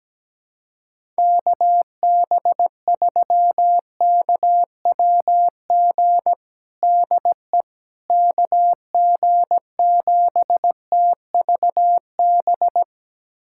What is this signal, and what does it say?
Morse code, heard as a keyed tone, about 17 wpm, 700 Hz. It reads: KB3KWG DE KG7TVB
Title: USA Callsign DE Callsign 17wpm